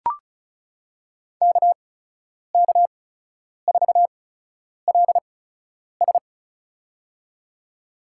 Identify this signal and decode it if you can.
Morse code, heard as a keyed tone, about 36 wpm, spaced as for 10 wpm, 700 Hz. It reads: KK4LS
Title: USA Call Signs - 8x Character Spacing - 35wpm